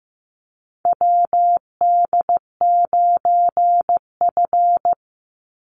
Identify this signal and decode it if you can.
Morse code, heard as a keyed tone, about 15 wpm, 700 Hz. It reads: WD9F
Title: NA SOTA Chaser Callsigns 15wpm